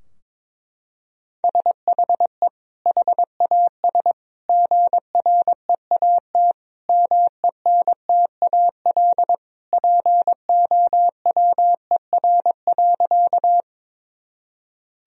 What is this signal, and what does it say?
Morse code, heard as a keyed tone, about 22 wpm, 700 Hz. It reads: SHE HAS GREAT MENTAL POWER.
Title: Sentences from Top 1000 Words - Review Part 4 - 22wpm